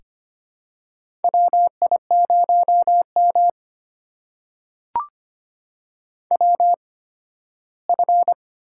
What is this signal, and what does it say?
Morse code, heard as a keyed tone, about 25 wpm, spaced as for 15 wpm, 700 Hz. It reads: WI0M WF
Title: USA Call Signs - 4 Characters - 8x Character Spacing - 25wpm